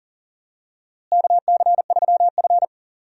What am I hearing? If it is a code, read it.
Morse code, heard as a keyed tone, about 40 wpm, 700 Hz. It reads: KC3F
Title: USA Call Signs - 4 Characters - Repeated 3x - 40wpm